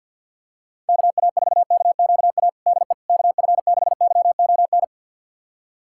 Morse code, heard as a keyed tone, about 52 wpm, 700 Hz. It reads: KA4KQA DE KF6YON
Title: USA Callsign DE Callsign 50wpm